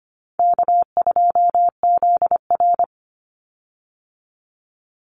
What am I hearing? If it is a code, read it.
Morse code, heard as a keyed tone, about 25 wpm, 700 Hz. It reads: K2ZR
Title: Easy Call Signs - 25wpm